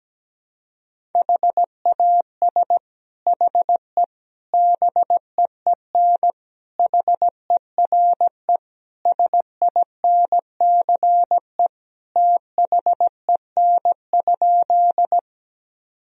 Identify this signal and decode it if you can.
Morse code, heard as a keyed tone, about 17 wpm, 700 Hz. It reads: HAS HE BEEN HERE SINCE THEN?
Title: Sentences from Top 1000 Words - Review Part 1 - 17wpm